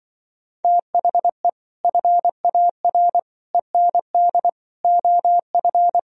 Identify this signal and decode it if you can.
Morse code, heard as a keyed tone, about 24 wpm, 700 Hz. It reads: THE FAR END OF
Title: Speed-Racing - Sets of 5 Words 20wpm